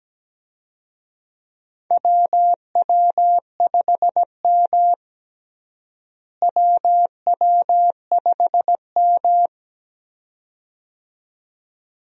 Morse code, heard as a keyed tone, about 17 wpm, 700 Hz. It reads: WW5M WW5M
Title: USA Call Signs - 4 Characters - Repeated 2x - 17wpm